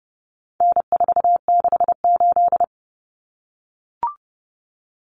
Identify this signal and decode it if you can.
Morse code, heard as a keyed tone, about 30 wpm, 700 Hz. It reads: N468
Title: Common Cut Numbers - 4-Digit 30wpm